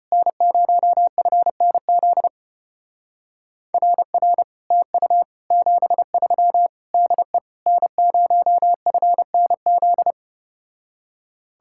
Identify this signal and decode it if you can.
Morse code, heard as a keyed tone, about 33 wpm, 700 Hz. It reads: N0FNZ RR TU 73 DE N0FNZ